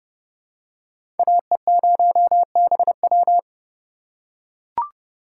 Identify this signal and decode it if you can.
Morse code, heard as a keyed tone, about 30 wpm, 700 Hz. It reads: AE0BW
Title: USA Call Signs - 5 Characters - 2x Character Spacing - 30wpm